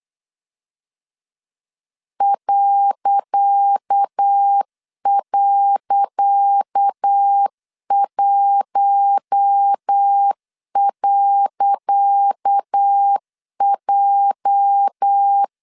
Morse code, heard as a keyed tone, about 8 wpm, 800 Hz. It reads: ..1.J
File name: LEZIONI / 10 SIGNS / 10C DOT 8wpm 3649